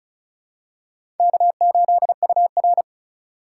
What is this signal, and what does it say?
Morse code, heard as a keyed tone, about 36 wpm, 700 Hz. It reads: K8UR
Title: USA Call Signs - 4 Characters - 4x Character Spacing - 35wpm